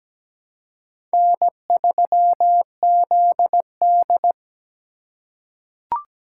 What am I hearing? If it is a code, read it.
Morse code, heard as a keyed tone, about 17 wpm, 700 Hz. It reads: N3ZD
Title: USA Call Signs - 4 Characters - 3x Character Spacing - 17wpm